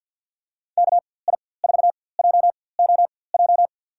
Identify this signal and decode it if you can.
Morse code, heard as a keyed tone, about 46 wpm, spaced as for 23 wpm, 700 Hz. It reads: KI4JOJ
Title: USA Call Signs - 6 Characters - 4x Character Spacing - 50wpm